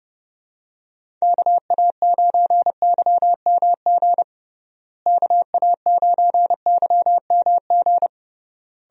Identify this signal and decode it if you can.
Morse code, heard as a keyed tone, about 30 wpm, 700 Hz. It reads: KA9YMG KA9YMG
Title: USA Call Signs - 6 Characters - Repeated 3x - 30wpm